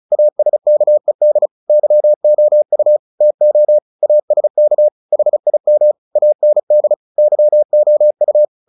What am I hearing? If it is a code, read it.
Morse code, heard as a keyed tone, about 35 wpm, 600 Hz. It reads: ASKED YOU TO ASK HIM AND YOU
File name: the old man and the sea / the old man and the sea 1 35wpm